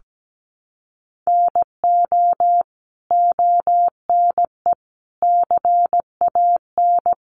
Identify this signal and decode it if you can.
Morse code, heard as a keyed tone, about 17 wpm, 700 Hz. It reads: NO ONE CAN
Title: Sentences from Top 1300 Words 17wpm